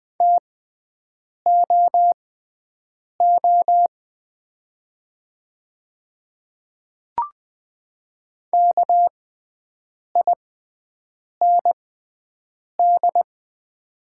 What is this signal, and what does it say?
Morse code, heard as a keyed tone, about 20 wpm, spaced as for 7 wpm, 700 Hz. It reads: TOO KIND
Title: 6x Character Spacing - Top 300 Words - 20wpm